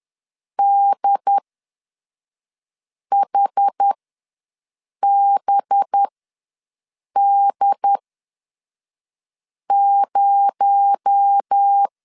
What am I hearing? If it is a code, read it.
Morse code, heard as a keyed tone, about 10 wpm, spaced as for 5 wpm, 800 Hz. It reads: DHBD0
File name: LEZIONI / 6 PKX0 / 6 PKX0 ws 10wpm 4220